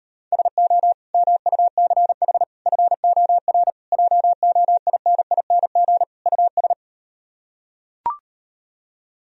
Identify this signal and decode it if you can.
Morse code, heard as a keyed tone, about 38 wpm, 700 Hz. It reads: SO MUCH FOR JOINING US